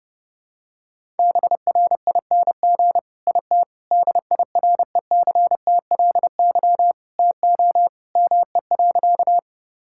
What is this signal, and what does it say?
Morse code, heard as a keyed tone, about 30 wpm, 700 Hz. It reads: BRING IT DIRECTLY TO ME.
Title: Sentences from Top 1000 Words - Review Part 5 - 30wpm